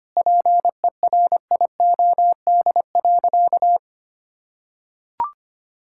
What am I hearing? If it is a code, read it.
Morse code, heard as a keyed tone, about 25 wpm, 700 Hz. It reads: PERIOD.
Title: Sentences from Top 1000 Words - Review Part 5 - 25wpm